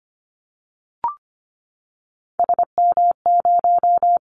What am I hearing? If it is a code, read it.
Morse code, heard as a keyed tone, about 25 wpm, 700 Hz. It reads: SM0
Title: Callsign Prefixes for Top 25 DXCC - Repeated 3x - 25wpm